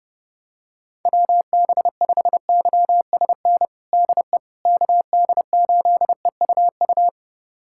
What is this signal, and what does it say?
Morse code, heard as a keyed tone, about 30 wpm, 700 Hz. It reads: WB5YSN DE KD8EUU
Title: USA Callsign DE Callsign - 6 Characters - 30wpm